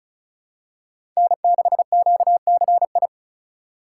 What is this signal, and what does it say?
Morse code, heard as a keyed tone, about 36 wpm, 700 Hz. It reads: N6QCI
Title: USA Call Signs - 5 Characters - 2x Character Spacing - 35wpm